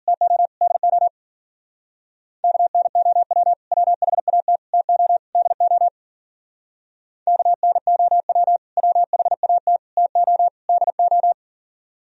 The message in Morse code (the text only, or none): TO DO KNOW WHAT TO DO KNOW WHAT TO DO